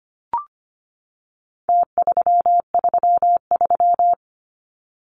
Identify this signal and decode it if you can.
Morse code, heard as a keyed tone, about 25 wpm, 700 Hz. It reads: T333
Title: Numbers - Contest Style - 4-Digit 25wpm